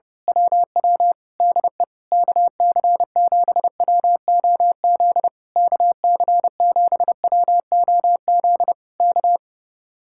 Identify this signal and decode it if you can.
Morse code, heard as a keyed tone, about 25 wpm, 700 Hz. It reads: WW DE KC7WOZ KC7WOZ K
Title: Speed-Racing Common QSO Phrases - 25wpm